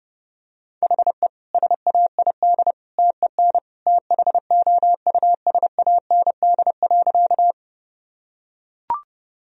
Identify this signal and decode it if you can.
Morse code, heard as a keyed tone, about 30 wpm, 700 Hz. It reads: HE SAID TEN THOUSAND.
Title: Sentences from Top 1000 Words - Review Part 3 - 30wpm